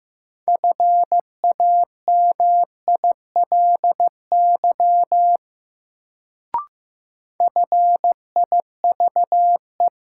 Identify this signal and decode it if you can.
Morse code, heard as a keyed tone, about 15 wpm, 700 Hz. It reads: FAMILY FIVE
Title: Sentences from Top 400 Words - New Word Drill - 15wpm